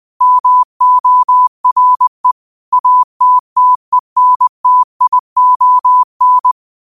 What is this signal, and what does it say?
Morse code, heard as a keyed tone, about 20 wpm, 1000 Hz. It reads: MORE ATTENTION
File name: qotdfiles / 20wpm